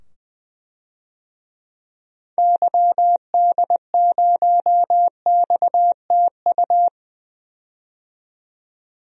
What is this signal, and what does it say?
Morse code, heard as a keyed tone, about 20 wpm, 700 Hz. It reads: YD0XTU
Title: DXCC-100 Call Signs - 6 Characters - 20wpm